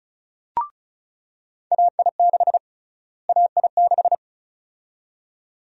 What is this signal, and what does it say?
Morse code, heard as a keyed tone, about 36 wpm, 700 Hz. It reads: AI6 AI6
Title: USA Call Sign Prefixes - Repeated 2x - 35wpm